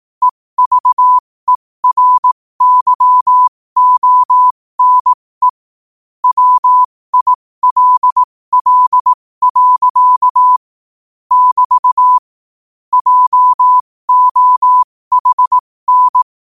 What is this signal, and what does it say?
Morse code, heard as a keyed tone, about 18 wpm, spaced as for 15 wpm, 1000 Hz. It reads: EVERYONE WILL. <BT> JOHN
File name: 15wpm